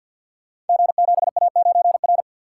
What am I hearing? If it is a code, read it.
Morse code, heard as a keyed tone, about 48 wpm, 690 Hz. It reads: G7A9R
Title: Code Groups - 5 Letter:Numbers - 50wpm